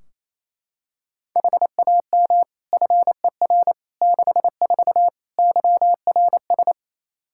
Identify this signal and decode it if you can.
Morse code, heard as a keyed tone, about 28 wpm, 700 Hz. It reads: HAM FER 64 YRS
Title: K1USN SST QSO Practice 28wpm